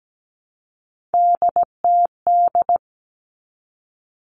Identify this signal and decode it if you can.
Morse code, heard as a keyed tone, about 17 wpm, 700 Hz. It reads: DTD
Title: Three Challenging Letter-Number Combinations 17wpm